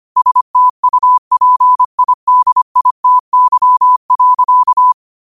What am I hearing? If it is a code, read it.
Morse code, heard as a keyed tone, about 25 wpm, 1000 Hz. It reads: STUPIDITY.